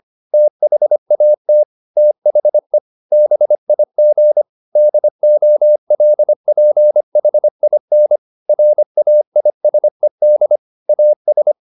Morse code, heard as a keyed tone, about 25 wpm, 600 Hz. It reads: THAT THE BIG DOLPHIN RAISED AS
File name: the old man and the sea part 1 25wpm